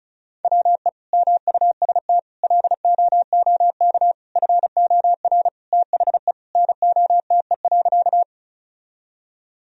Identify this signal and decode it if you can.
Morse code, heard as a keyed tone, about 36 wpm, 700 Hz. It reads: WE MUST LOOK FOR THE NOTE.
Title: Sentences from Top 1000 Words - Review Part 2 - 35wpm